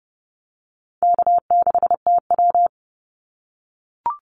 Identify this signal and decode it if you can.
Morse code, heard as a keyed tone, about 30 wpm, 700 Hz. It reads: K6TW
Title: NA SOTA Activator Callsigns 30wpm